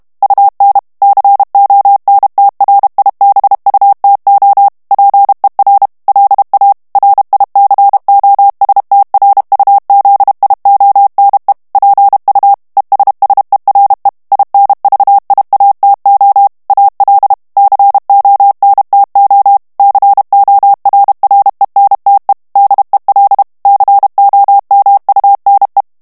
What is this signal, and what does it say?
Morse code, heard as a keyed tone, about 32 wpm, 800 Hz. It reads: UN CONTRIBUTO PER LA RICOSTRUZIONE PU ESSERE INVIATO AL CONTO CORRENTE DEL COMUNE
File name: morse mp3 / ITALIEN texte clair 1 vitesse 30 mots minute